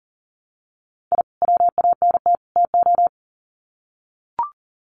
Morse code, heard as a keyed tone, about 40 wpm, 700 Hz. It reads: I WANT TO
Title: Sets of 3 Words 40wpm